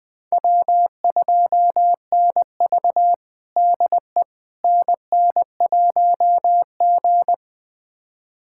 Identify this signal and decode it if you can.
Morse code, heard as a keyed tone, about 20 wpm, 700 Hz. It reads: W2NV DE NN1G
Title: USA Callsign DE Callsign - 4 Characters - 20wpm